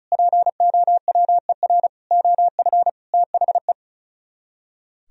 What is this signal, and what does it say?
Morse code, heard as a keyed tone, about 45 wpm, 700 Hz. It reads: POWER OF THE